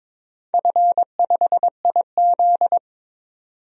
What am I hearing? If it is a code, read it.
Morse code, heard as a keyed tone, about 22 wpm, 700 Hz. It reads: F5IZ
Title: DXCC-100 Call Signs - 4 Characters - 22wpm